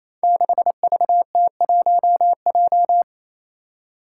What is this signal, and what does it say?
Morse code, heard as a keyed tone, about 27 wpm, 700 Hz. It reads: 6VT1J
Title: Five Challenging Letter-Number Combinations 28wpm